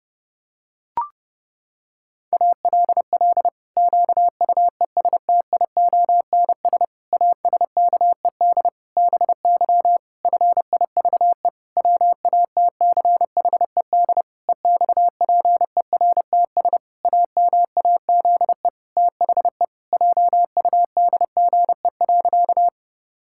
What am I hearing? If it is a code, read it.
Morse code, heard as a keyed tone, about 30 wpm, 700 Hz. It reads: ALL QUESTIONS ASKED BY FIVE WATCHED EXPERTS AMAZE THE JUDGE.
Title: Pangram Challenge v2 - 30wpm